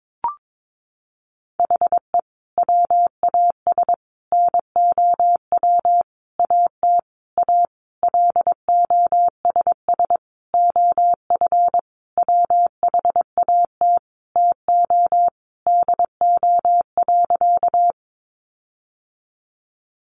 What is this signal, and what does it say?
Morse code, heard as a keyed tone, about 22 wpm, 700 Hz. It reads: HE WAS NOW AT A LOSS OF WHAT TO DO.
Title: Sentences from Top 1400 Words 22wpm